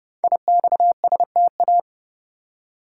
Morse code, heard as a keyed tone, about 30 wpm, 690 Hz. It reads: IXSTA
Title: Code Groups - 5 Letters - 30wpm